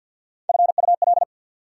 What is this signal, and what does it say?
Morse code, heard as a keyed tone, about 48 wpm, 690 Hz. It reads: FUL
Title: Prefixes and Suffixes 50wpm